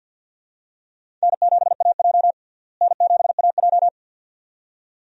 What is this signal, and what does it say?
Morse code, heard as a keyed tone, about 53 wpm, 690 Hz. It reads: N7AJ N7AJ
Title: USA Call Signs - 4 Characters - Repeated 2x - 50wpm